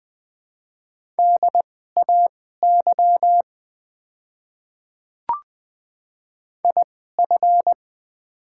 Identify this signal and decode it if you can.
Morse code, heard as a keyed tone, about 20 wpm, spaced as for 14 wpm, 700 Hz. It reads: DAY IF
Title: 2x Character Spacing - Top 100 Words - 20wpm